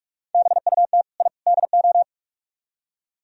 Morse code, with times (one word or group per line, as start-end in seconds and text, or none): 0.3-1.0 BUT
1.2-1.3 I
1.5-2.0 DO